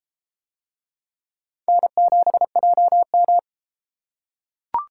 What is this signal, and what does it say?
Morse code, heard as a keyed tone, about 32 wpm, 700 Hz. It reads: N7JM